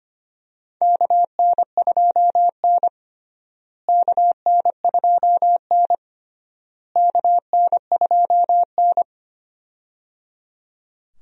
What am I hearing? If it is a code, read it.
Morse code, heard as a keyed tone, about 25 wpm, 700 Hz. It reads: KN2N KN2N KN2N
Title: USA Call Signs - 4 Characters - Repeated 3x - 25wpm